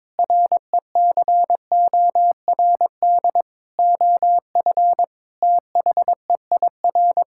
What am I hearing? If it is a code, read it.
Morse code, heard as a keyed tone, about 22 wpm, 700 Hz. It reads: RECORD OF THEIR
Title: Sentences from Top 1000 Words - Review Part 5 - 22wpm